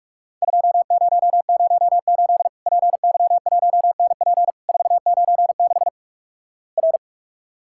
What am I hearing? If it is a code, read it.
Morse code, heard as a keyed tone, about 46 wpm, 700 Hz. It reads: 2008 PY1NP 496 R
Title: Worked All Europe DX Contest - QTC Exchange - 45wpm